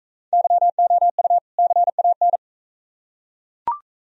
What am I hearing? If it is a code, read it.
Morse code, heard as a keyed tone, about 39 wpm, 700 Hz. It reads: YOU CAN